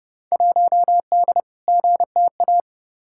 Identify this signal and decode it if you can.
Morse code, heard as a keyed tone, about 30 wpm, 700 Hz. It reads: 1D GTA
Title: Field Day Exchanges - 30wpm